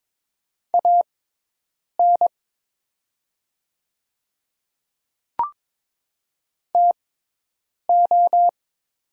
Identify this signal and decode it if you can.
Morse code, heard as a keyed tone, about 22 wpm, spaced as for 8 wpm, 690 Hz. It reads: AN TO
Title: 6x Character Spacing - Top 100 Words - 22wpm